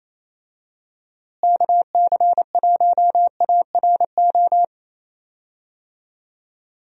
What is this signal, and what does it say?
Morse code, heard as a keyed tone, about 28 wpm, 700 Hz. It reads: KC1ARO
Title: USA Call Signs - 6 Characters - Repeated 3x - 28wpm